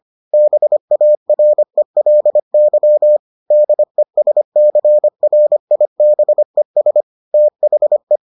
Morse code, heard as a keyed tone, about 25 wpm, 600 Hz. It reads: BARELY DESCRIBES THE